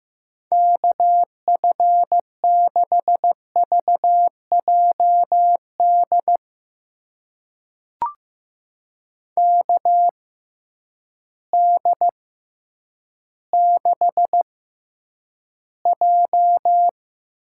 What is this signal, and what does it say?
Morse code, heard as a keyed tone, about 15 wpm, spaced as for 9 wpm, 700 Hz. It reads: KF6VJD KD6J
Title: USA Call Signs - 6 Characters - 6x Character Spacing - 15wpm